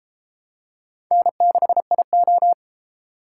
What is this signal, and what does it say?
Morse code, heard as a keyed tone, about 31 wpm, 700 Hz. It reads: N6IO